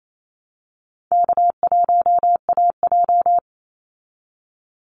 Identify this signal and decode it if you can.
Morse code, heard as a keyed tone, about 28 wpm, 700 Hz. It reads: K1AJ
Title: Easy Call Signs Repeated 3x - 28wpm